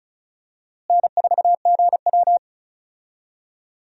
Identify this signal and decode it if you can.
Morse code, heard as a keyed tone, about 34 wpm, 700 Hz. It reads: N4GW
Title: USA Call Signs - 4 Characters - Repeated 2x - 35wpm